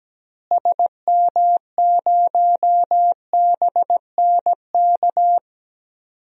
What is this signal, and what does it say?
Morse code, heard as a keyed tone, about 17 wpm, 700 Hz. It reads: SM0BNK
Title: DXCC-100 Call Signs - 6 Characters - Repeated 3x - 17wpm